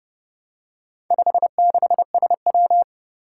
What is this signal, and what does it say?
Morse code, heard as a keyed tone, about 30 wpm, 690 Hz. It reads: 56SW
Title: Four Challenging Letter-Number Combinations 30wpm